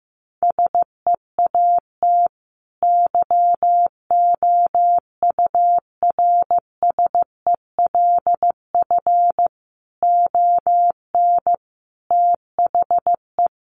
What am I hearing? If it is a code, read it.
Morse code, heard as a keyed tone, about 15 wpm, 700 Hz. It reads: SEAT YOURSELF ON THE
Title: Sentences from Top 1500 Words 15wpm